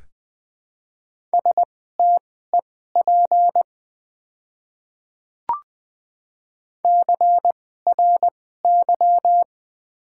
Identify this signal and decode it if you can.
Morse code, heard as a keyed tone, about 20 wpm, spaced as for 15 wpm, 700 Hz. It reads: STEP CRY
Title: 2x Character Spacing - Top 500 Words - 20wpm